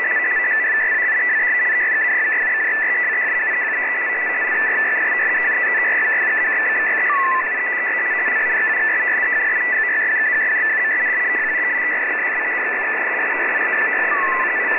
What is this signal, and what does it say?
Two-tone teleprinter signal, about 200 Hz shift.